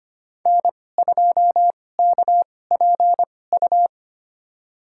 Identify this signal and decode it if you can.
Morse code, heard as a keyed tone, about 25 wpm, spaced as for 18 wpm, 700 Hz. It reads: N2KPU
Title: USA Call Signs - 2x Character Spacing - 25wpm